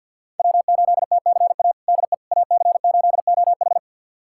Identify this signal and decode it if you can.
Morse code, heard as a keyed tone, about 52 wpm, 690 Hz. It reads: W8TCA DE AC8OH
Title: USA Callsign DE Callsign - 5 Characters - 50wpm